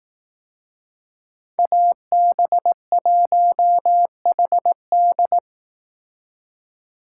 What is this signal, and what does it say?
Morse code, heard as a keyed tone, about 18 wpm, 700 Hz. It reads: AB1HD
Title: Speed Racing - USA 2x2 Call Signs 15wpm